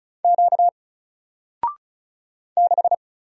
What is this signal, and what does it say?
Morse code, heard as a keyed tone, about 36 wpm, 700 Hz. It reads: Q 6
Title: Single Letter-Number Alt 35wpm